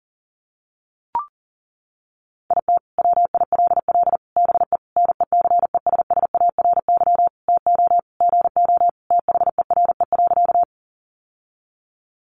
Morse code, heard as a keyed tone, about 40 wpm, 700 Hz. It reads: IT WILL BE NECESSARY TO GO THERE.
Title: Sentences from Top 800 Words 40wpm